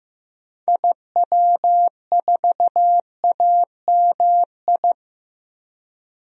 Morse code, heard as a keyed tone, about 15 wpm, 700 Hz. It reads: IW4AMI
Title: DXCC-100 Call Signs - 6 Characters - Repeated 3x - 15wpm